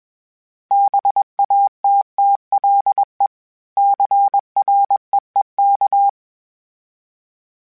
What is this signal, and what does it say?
Morse code, heard as a keyed tone, about 21 wpm, 800 Hz. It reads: BATTLE CREEK
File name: fromSMs / FromSMs mp3s / Cities@20